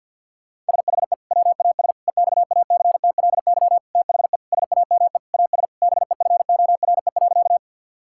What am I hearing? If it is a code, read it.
Morse code, heard as a keyed tone, about 50 wpm, 700 Hz. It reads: SHE WAS EXACTLY THE SAME AS BEFORE.
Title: Sentences from Top 1000 Words - Review Part 6 - 50wpm